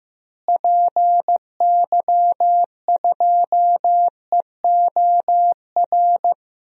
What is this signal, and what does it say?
Morse code, heard as a keyed tone, about 15 wpm, 700 Hz. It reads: PY2EOR
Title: DXCC-100 Call Signs - 6 Characters - Repeated 2x - 15wpm